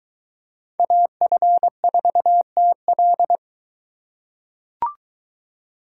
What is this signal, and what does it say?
Morse code, heard as a keyed tone, about 21 wpm, 700 Hz. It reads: AF4TL